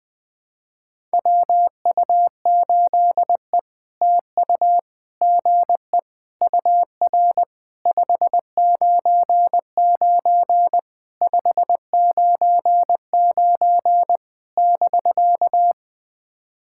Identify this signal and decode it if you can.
Morse code, heard as a keyed tone, about 20 wpm, 700 Hz. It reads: WU8E TU GE UR 599 599 <BK>
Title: POTA QSO Practice - 20wpm